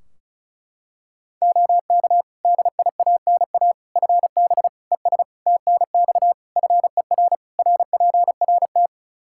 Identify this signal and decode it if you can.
Morse code, heard as a keyed tone, about 36 wpm, 700 Hz. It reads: OK DIANA FB ES TNX FER RPRT
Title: Practice QSO - Exchange 2 - Info - 35wpm